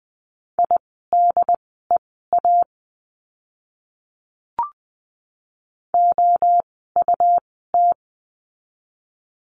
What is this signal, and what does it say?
Morse code, heard as a keyed tone, about 20 wpm, spaced as for 15 wpm, 700 Hz. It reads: IDEA OUT
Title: 108 Lesson 18f - Letter m - Words 20wpm